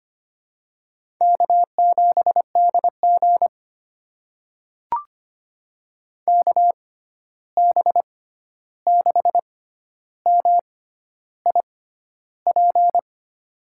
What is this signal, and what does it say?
Morse code, heard as a keyed tone, about 25 wpm, spaced as for 11 wpm, 700 Hz. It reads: K7DG KB6MIP
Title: USA Call Signs - 6x Character Spacing - 25wpm